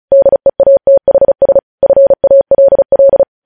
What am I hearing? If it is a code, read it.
Morse code, heard as a keyed tone, about 36 wpm, 550 Hz. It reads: DEATHS FALL
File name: podcast / News headlines-1x-35x35